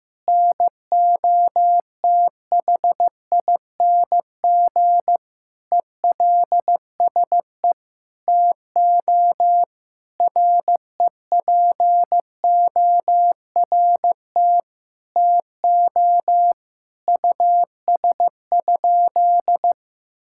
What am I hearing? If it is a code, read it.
Morse code, heard as a keyed tone, about 16 wpm, 700 Hz. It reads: NOTHING ELSE TO REPORT TO US?
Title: Sentences from Top 1000 Words - Review Part 6 - 15wpm